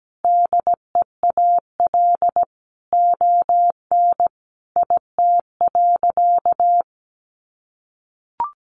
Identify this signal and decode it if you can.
Morse code, heard as a keyed tone, about 17 wpm, 700 Hz. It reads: DEAL ON IT.
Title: Sentences from Top 1300 Words 17wpm